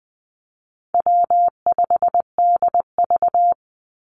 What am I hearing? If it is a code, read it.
Morse code, heard as a keyed tone, about 20 wpm, 700 Hz. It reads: W5DV
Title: Call Signs Repeated 3x - 20wpm